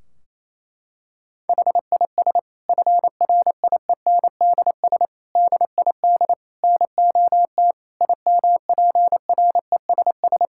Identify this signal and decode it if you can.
Morse code, heard as a keyed tone, about 28 wpm, 700 Hz. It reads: HIS FRIENDS DID NOT IMPRESS
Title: Sentences from Top 1600 Words 28wpm